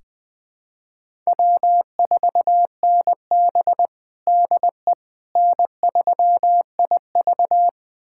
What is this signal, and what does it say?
Morse code, heard as a keyed tone, about 20 wpm, 700 Hz. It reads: W4NB DE N3IV
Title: USA Callsign DE Callsign - 4 Characters - 20wpm